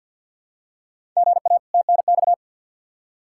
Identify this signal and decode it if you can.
Morse code, heard as a keyed tone, about 53 wpm, 700 Hz. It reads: GA TNX
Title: Practice QSO - Exchange 1 - Beginning - 50wpm